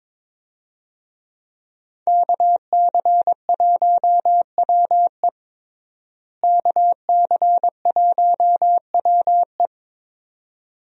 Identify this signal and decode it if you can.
Morse code, heard as a keyed tone, about 22 wpm, 700 Hz. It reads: KC1WE KC1WE
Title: USA Call Signs - 5 Characters - Repeated 2x - 22wpm